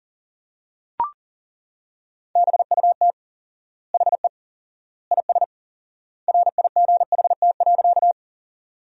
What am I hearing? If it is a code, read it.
Morse code, heard as a keyed tone, about 40 wpm, 700 Hz. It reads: BUT HE IS RIGHT.
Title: 4x Word-Spacing - Sentences from Top 200 Words - 40wpm